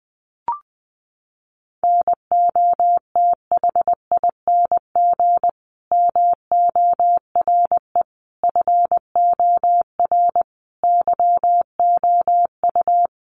Morse code, heard as a keyed tone, about 20 wpm, 700 Hz. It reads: NOTHING MORE FOR YOU
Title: Sentences from Top 1100 Words 20wpm